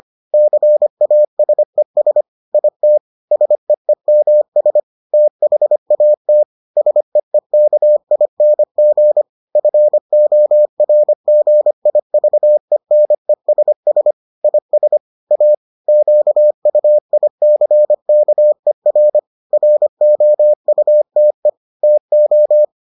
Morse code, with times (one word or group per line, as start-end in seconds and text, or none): 0.3-2.2 CASES
2.5-3.0 IT
3.3-4.8 SEEMS
5.1-6.4 THAT
6.8-9.2 SEEKING
9.6-14.1 FORGIVENESS
14.4-15.0 IS
15.3-15.5 A
15.9-19.2 QUICKER
19.5-21.5 ROUTE
21.8-22.6 TO